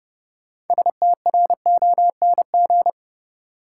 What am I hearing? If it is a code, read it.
Morse code, heard as a keyed tone, about 30 wpm, 700 Hz. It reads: STRONG
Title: Sentences from Top 500 Words - New Word Drill - 30wpm